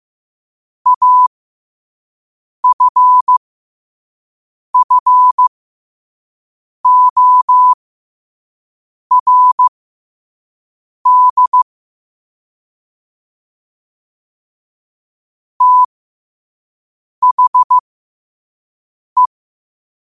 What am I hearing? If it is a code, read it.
Morse code, heard as a keyed tone, about 15 wpm, spaced as for 5 wpm, 1000 Hz. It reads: AFFORD THE
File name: qotdfiles / 5wpm